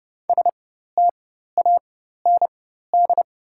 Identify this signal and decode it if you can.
Morse code, heard as a keyed tone, about 31 wpm, spaced as for 14 wpm, 700 Hz. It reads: STAND
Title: 4x Character Spacing - Top 500 Words - 30wpm